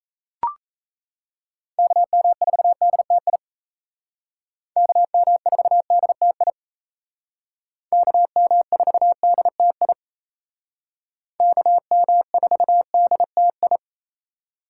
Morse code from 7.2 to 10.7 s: KM4DTI

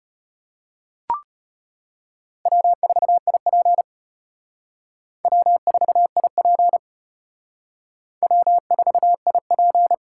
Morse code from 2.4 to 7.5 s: W4IP W4IP

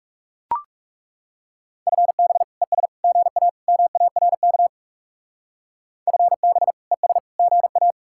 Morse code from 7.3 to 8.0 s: GA

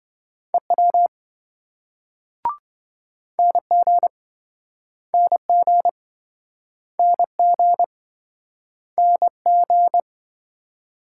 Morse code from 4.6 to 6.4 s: NG

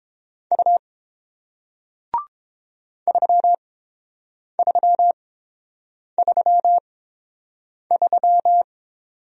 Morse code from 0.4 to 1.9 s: U

